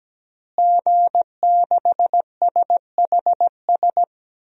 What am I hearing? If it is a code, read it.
Morse code, heard as a keyed tone, about 17 wpm, 700 Hz. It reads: G6SHS
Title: DXCC-100 Call Signs - 5 Characters - Repeated 3x - 17wpm